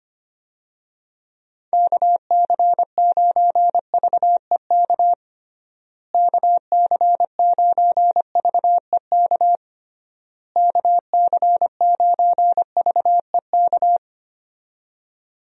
Morse code, heard as a keyed tone, about 25 wpm, 700 Hz. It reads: KC9VEK KC9VEK KC9VEK
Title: USA Call Signs - 6 Characters - Repeated 3x - 25wpm